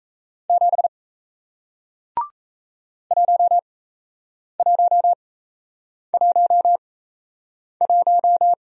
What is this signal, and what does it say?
Morse code, heard as a keyed tone, about 38 wpm, spaced as for 8 wpm, 700 Hz. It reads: 7 1111